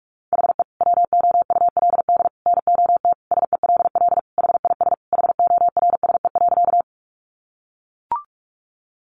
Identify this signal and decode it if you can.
Morse code, heard as a keyed tone, about 44 wpm, 700 Hz. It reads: HE WOULD NOT SELL HIS HORSE.
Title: Sentences from Top 1100 Words 45wpm